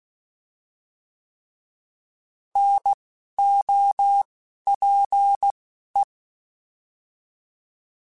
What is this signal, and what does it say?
Morse code, heard as a keyed tone, about 16 wpm, spaced as for 11 wpm, 800 Hz. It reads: NOPE